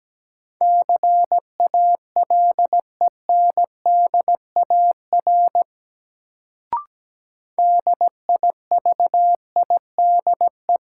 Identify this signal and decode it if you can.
Morse code, heard as a keyed tone, about 17 wpm, 700 Hz. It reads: CALENDAR DIVIDE
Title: Sentences from Top 1600 Words - New Word Drill - 17wpm